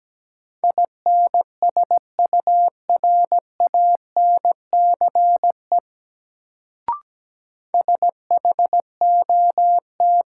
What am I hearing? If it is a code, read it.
Morse code, heard as a keyed tone, about 17 wpm, 700 Hz. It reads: INSURANCE SHOT
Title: Sentences from Top 1100 Words - New Word Drill - 17wpm